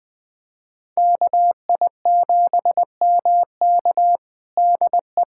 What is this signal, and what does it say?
Morse code, heard as a keyed tone, about 20 wpm, 700 Hz. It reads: KI7MK DE
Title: USA Callsign DE Callsign - 5 Characters - 20wpm